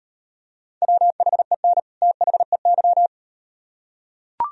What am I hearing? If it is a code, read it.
Morse code, heard as a keyed tone, about 38 wpm, 700 Hz. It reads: WHEN THEY